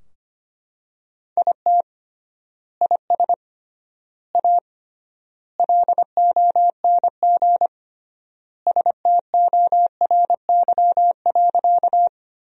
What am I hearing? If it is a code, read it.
Morse code, heard as a keyed tone, about 25 wpm, 700 Hz. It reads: IT IS A LONG STORY.
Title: 3x Word-Spacing - Sentences from Top 300 Words - 25wpm